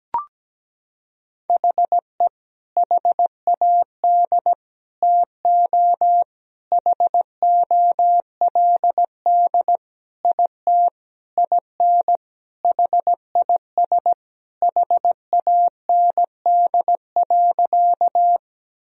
Morse code, heard as a keyed tone, about 17 wpm, 700 Hz. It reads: HE HAD TO HOLD IT IN HIS HAND.
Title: Sentences from Top 1000 Words - Review Part 2 - 17wpm